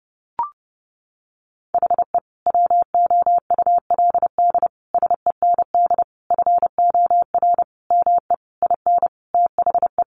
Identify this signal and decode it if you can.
Morse code, heard as a keyed tone, about 30 wpm, 700 Hz. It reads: HE WOULD SEND FOR ME IN THE
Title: Sentences from Top 1500 Words 30wpm